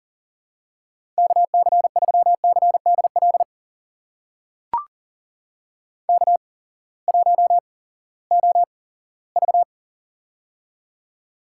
Morse code, heard as a keyed tone, about 39 wpm, spaced as for 20 wpm, 700 Hz. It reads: KC3CDL K1OV
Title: USA Call Signs - 8x Character Spacing - 40wpm